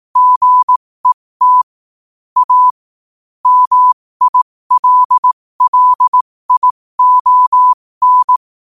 Morse code, heard as a keyed tone, about 18 wpm, spaced as for 15 wpm, 1000 Hz. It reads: GET A MILLION
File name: qotdfiles / 15wpm